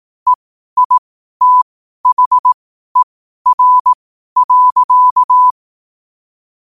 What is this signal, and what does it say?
Morse code, heard as a keyed tone, about 18 wpm, spaced as for 12 wpm, 1000 Hz. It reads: EITHER.
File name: qotdfiles / 12wpm